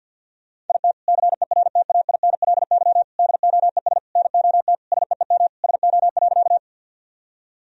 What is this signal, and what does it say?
Morse code, heard as a keyed tone, about 51 wpm, 700 Hz. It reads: IT CERTAINLY DOES NOT SEEM SO.
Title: Sentences from Top 1000 Words - Review Part 1 - 50wpm